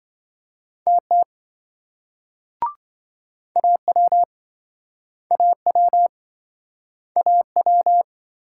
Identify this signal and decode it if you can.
Morse code, heard as a keyed tone, about 27 wpm, 700 Hz. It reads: TT AW AW AW